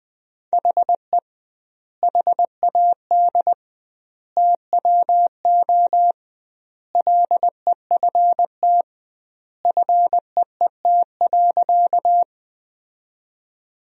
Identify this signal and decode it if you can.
Morse code, heard as a keyed tone, about 19 wpm, 700 Hz. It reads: HE HAD TWO LEFT FEET.
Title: 2x Word-Spacing - Sentences from Top 300 Words - 20wpm